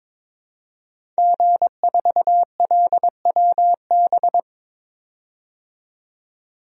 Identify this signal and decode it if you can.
Morse code, heard as a keyed tone, about 22 wpm, 700 Hz. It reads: G4LWB
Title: DXCC-100 Call Signs - 5 Characters - 22wpm